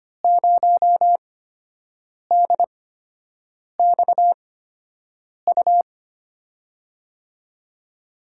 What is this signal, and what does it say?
Morse code, heard as a keyed tone, about 25 wpm, spaced as for 7 wpm, 700 Hz. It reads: 0DXU
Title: USA Call Signs - 8x Character Spacing - 25wpm